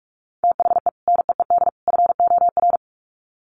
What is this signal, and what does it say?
Morse code, heard as a keyed tone, about 44 wpm, 700 Hz. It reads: THE NEED FOR
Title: Sets of 3 Words 45wpm